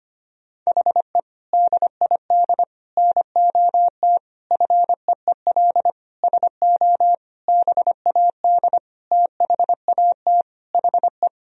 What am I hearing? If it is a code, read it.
Morse code, heard as a keyed tone, about 25 wpm, 700 Hz. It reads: HE DID NOT FEEL SO BAD THAT HE
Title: Sentences from Top 1000 Words - Review Part 4 - 25wpm